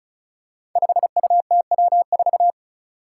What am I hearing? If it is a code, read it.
Morse code, heard as a keyed tone, about 36 wpm, 700 Hz. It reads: 5UTW4
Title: Five Challenging Letter-Number Combinations 35wpm